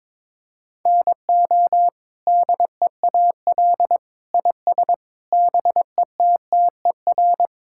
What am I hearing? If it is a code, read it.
Morse code, heard as a keyed tone, about 22 wpm, 700 Hz. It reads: NO DEAL IS BETTER
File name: practice / 22wpm / Sets of 8 Words - 22wpm